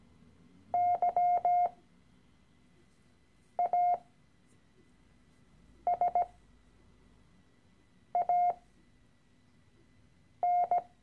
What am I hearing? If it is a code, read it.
Morse code, heard as a keyed tone, about 17 wpm, spaced as for 4 wpm, 710 Hz. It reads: YASAN
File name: CW-Online-Kurs / Anfänger / Lektion 013